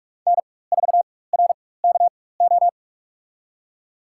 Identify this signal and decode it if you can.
Morse code, heard as a keyed tone, about 46 wpm, spaced as for 21 wpm, 700 Hz. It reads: N4RKO
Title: USA Call Signs - 5 Characters - 4x Character Spacing - 45wpm